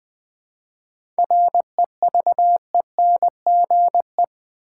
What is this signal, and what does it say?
Morse code, heard as a keyed tone, about 20 wpm, 700 Hz. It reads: REVENGE
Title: 7 Letter Words - First 3 Letters Given - 20wpm